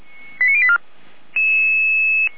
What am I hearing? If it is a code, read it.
Five-tone selective call (ZVEI), heard as quick runs of five tones, tones of about 70 ms. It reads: 89084
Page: SDR/ZVEI